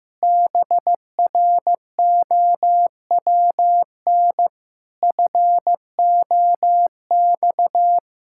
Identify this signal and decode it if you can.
Morse code, heard as a keyed tone, about 15 wpm, 700 Hz. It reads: BROWN FOX
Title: Pangram Challenge v2 - 15wpm